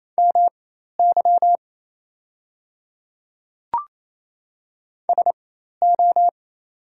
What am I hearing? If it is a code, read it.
Morse code, heard as a keyed tone, about 28 wpm, spaced as for 13 wpm, 700 Hz. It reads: MY SO
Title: 4x Character Spacing - Top 100 Words - 28wpm